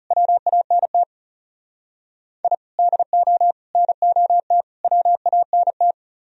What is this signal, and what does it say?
Morse code, heard as a keyed tone, about 38 wpm, 700 Hz. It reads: WANT I DO NOT WANT